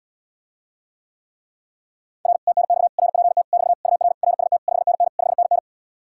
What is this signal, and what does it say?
Morse code, heard as a keyed tone, about 80 wpm, 700 Hz. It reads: R TNX RPT <BT> UR RST 5NN 5NN